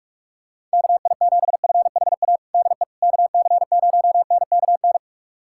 Morse code, heard as a keyed tone, about 43 wpm, 690 Hz. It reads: KI7FHA DE KC0NKN
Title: USA Callsign DE Callsign 45wpm